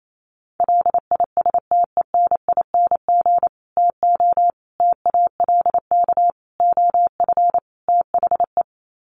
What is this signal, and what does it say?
Morse code, heard as a keyed tone, about 28 wpm, 700 Hz. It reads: LISTENING TO TALK OF THE
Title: Sets of 5 Words 28wpm